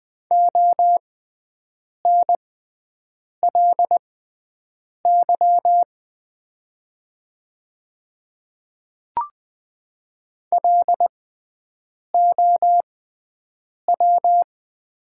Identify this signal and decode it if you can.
Morse code, heard as a keyed tone, about 20 wpm, spaced as for 7 wpm, 700 Hz. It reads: ONLY LOW
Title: 6x Character Spacing - Top 200 Words - 20wpm